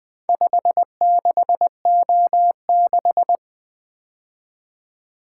Morse code, heard as a keyed tone, about 20 wpm, 700 Hz. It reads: 56O6
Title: Four Letter-Number Combinations - 20wpm